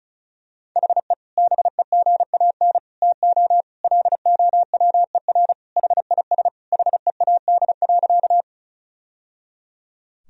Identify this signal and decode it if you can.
Morse code, heard as a keyed tone, about 36 wpm, 700 Hz. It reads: HE BEGAN TO LOWER HIS HEAD.
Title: Sentences from Top 1000 Words - Review Part 4 - 35wpm